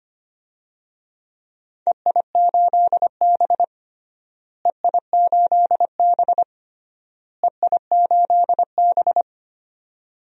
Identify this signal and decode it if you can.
Morse code, heard as a keyed tone, about 25 wpm, 700 Hz. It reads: EI8B EI8B EI8B
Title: DXCC-100 Call Signs - 4 Characters - Repeated 3x - 25wpm